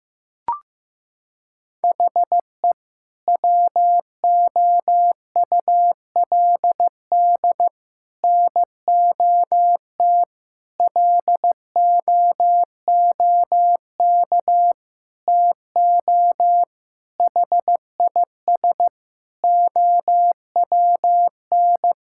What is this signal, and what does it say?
Morse code, heard as a keyed tone, about 16 wpm, 700 Hz. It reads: HE WOULD NOT LOOK TO HIS OWN
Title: Sentences from Top 1000 Words - Review Part 6 - 15wpm